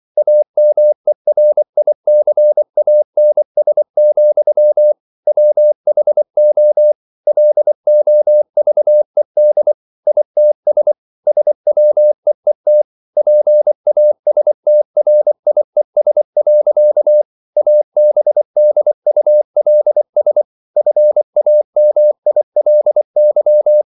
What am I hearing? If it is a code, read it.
Morse code, heard as a keyed tone, about 24 wpm, 600 Hz. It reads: AMERICANS, WHO LOVED ITS SWEET PASTRIES. ABDULS FAMILY